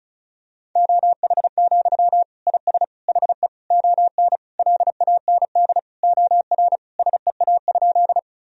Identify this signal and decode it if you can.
Morse code, heard as a keyed tone, about 35 wpm, 700 Hz. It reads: OH, IS HE ON LAND OR SEA?
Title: Sentences from Top 1000 Words - Review Part 1 - 35wpm